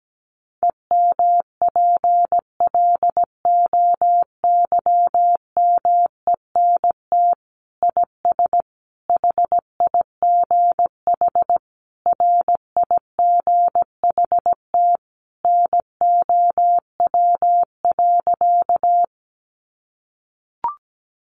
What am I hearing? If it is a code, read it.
Morse code, heard as a keyed tone, about 17 wpm, 700 Hz. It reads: EMPLOYMENT IS HIGH RIGHT NOW.
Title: Sentences from Top 1500 Words 17wpm